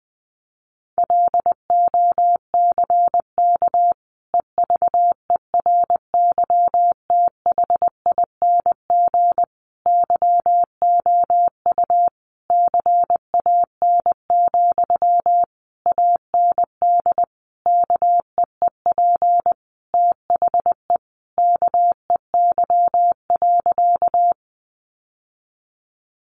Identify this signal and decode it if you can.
Morse code, heard as a keyed tone, about 20 wpm, 700 Hz. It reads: LOCK EVERYTHING YOU CAN, AND KEEP THE KEY.
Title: Sentences from Top 1300 Words 20wpm